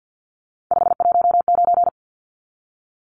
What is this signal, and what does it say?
Morse code, heard as a keyed tone, about 49 wpm, 700 Hz. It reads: 519
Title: Numbers - 3-Digit 50wpm